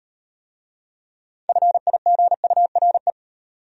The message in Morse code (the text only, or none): FIGURE